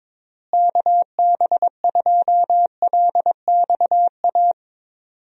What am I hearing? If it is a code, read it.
Morse code, heard as a keyed tone, about 22 wpm, 700 Hz. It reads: KB2LXA
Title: USA Call Signs - 6 Characters - Repeated 3x - 22wpm